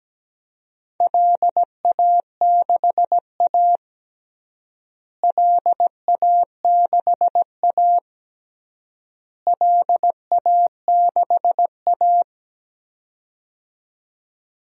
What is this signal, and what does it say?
Morse code, heard as a keyed tone, about 17 wpm, 700 Hz. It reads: LA6A LA6A LA6A
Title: DXCC-100 Call Signs - 4 Characters - Repeated 3x - 17wpm